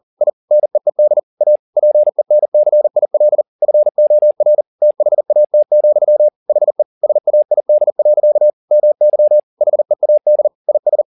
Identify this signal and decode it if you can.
Morse code, heard as a keyed tone, about 40 wpm, 600 Hz. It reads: I NEED A PENCIL FOR THAT, HE SAID. MY HEAD IS
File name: the old man and the sea 2 40wpm